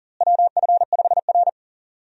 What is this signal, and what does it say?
Morse code, heard as a keyed tone, about 40 wpm, 700 Hz. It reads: WF5R
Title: USA Call Signs - 4 Characters - Repeated 3x - 40wpm